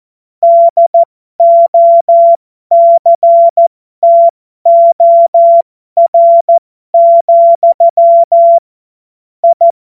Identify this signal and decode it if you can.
Morse code, heard as a keyed tone, about 14 wpm, spaced as for 12 wpm, 680 Hz. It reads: DOCTOR, I
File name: Morsekurs / Witze